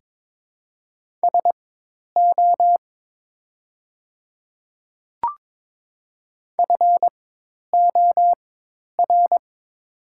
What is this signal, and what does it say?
Morse code, heard as a keyed tone, about 22 wpm, spaced as for 10 wpm, 700 Hz. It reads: SO FOR
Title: 4x Character Spacing - Top 100 Words - 22wpm